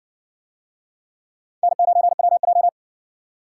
Morse code, heard as a keyed tone, about 60 wpm, 700 Hz. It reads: N9WJ